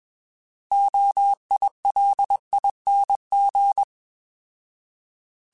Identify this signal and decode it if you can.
Morse code, heard as a keyed tone, about 21 wpm, 800 Hz. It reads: OILING
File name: Alphabetical / Alphabetical mp3s / Os @20